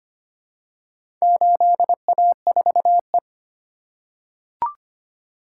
Morse code, heard as a keyed tone, about 25 wpm, 700 Hz. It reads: 8A4E
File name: practice / 25wpm / Four Letter-Number Combinations - 25wpm